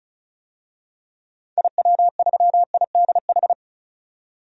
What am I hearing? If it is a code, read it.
Morse code, heard as a keyed tone, about 34 wpm, 700 Hz. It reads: IW3IDH
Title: DXCC-100 Call Signs - 6 Characters - 35wpm